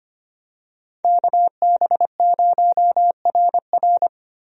Morse code, heard as a keyed tone, about 25 wpm, 700 Hz. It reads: KB0RR
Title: USA Call Signs - 5 Characters - 25wpm